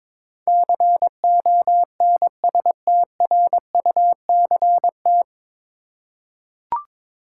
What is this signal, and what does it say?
Morse code, heard as a keyed tone, about 22 wpm, 700 Hz. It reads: CONSTRUCT
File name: 9 Letter Words - First 3 Letters Given - 22wpm